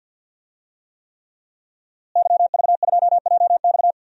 Morse code, heard as a keyed tone, about 50 wpm, 700 Hz. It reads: YV2JX
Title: DXCC-100 Call Signs - 5 Characters - Repeated 3x - 50wpm